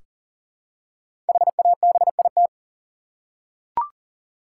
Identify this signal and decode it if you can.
Morse code, heard as a keyed tone, about 40 wpm, 700 Hz. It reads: HABIT
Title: Five Letter Words - 40wpm